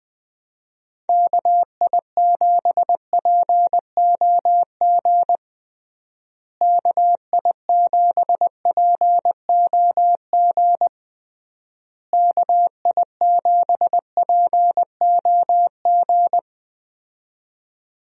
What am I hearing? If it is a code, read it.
Morse code, heard as a keyed tone, about 21 wpm, 700 Hz. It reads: KI7POG KI7POG KI7POG
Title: USA Call Signs - 6 Characters - Repeated 3x - 20wpm